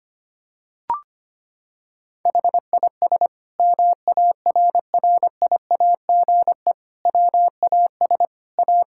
Morse code, heard as a keyed tone, about 25 wpm, 700 Hz. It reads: HIS MARRIAGE WAS A
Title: Sentences from Top 1000 Words - Review Part 5 - 25wpm